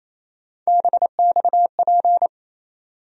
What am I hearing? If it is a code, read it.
Morse code, heard as a keyed tone, about 27 wpm, 700 Hz. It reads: BXP
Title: USA Call Sign Postfixes - 3 Letters 28wpm